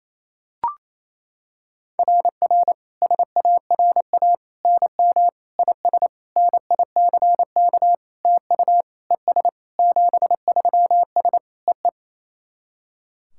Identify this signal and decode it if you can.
Morse code, heard as a keyed tone, about 28 wpm, 700 Hz. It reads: RR SARA NM IS NICK TU ES 73S EE
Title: SOTA QSO Practice 28wpm